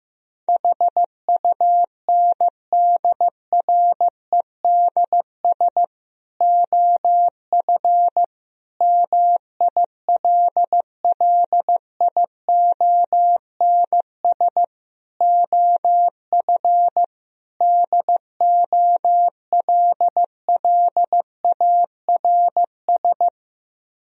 Morse code, heard as a keyed tone, about 15 wpm, 700 Hz. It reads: HUNDREDS OF MILLIONS OF DOLLARS
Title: Speed-Racing - Sets of 5 Words 15wpm